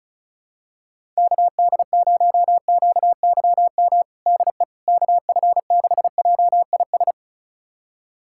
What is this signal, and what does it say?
Morse code, heard as a keyed tone, about 36 wpm, 700 Hz. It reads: KD0QYM DE KF6JIS
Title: USA Callsign DE Callsign - 6 Characters - 35wpm